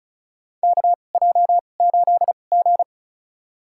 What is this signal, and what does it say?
Morse code, heard as a keyed tone, about 36 wpm, spaced as for 25 wpm, 700 Hz. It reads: KJ8G
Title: USA Call Signs - 4 Characters - 2x Character Spacing - 35wpm